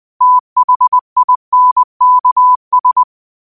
Morse code, heard as a keyed tone, about 20 wpm, 1000 Hz. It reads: THINKS
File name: qotdfiles / 20wpm